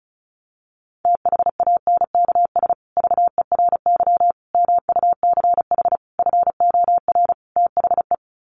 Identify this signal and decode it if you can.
Morse code, heard as a keyed tone, about 34 wpm, 700 Hz. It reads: THANKS VERY MUCH FOR THE
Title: Sets of 5 Words Encore2 35wpm